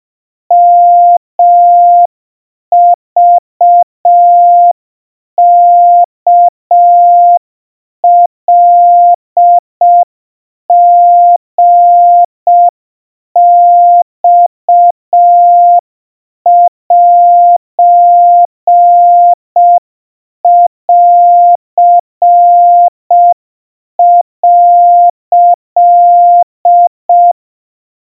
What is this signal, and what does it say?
Morse code (Japanese Wabun code), heard as a keyed tone, about 5 wpm, 700 Hz. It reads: ヨクワカリマセン」
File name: 20 cpm 700 Hz Crispy